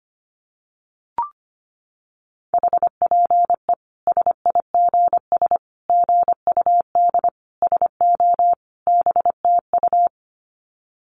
Morse code, heard as a keyed tone, about 25 wpm, 700 Hz. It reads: HPE SIGS GUD SO BTU
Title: Common QSO Phrases 25wpm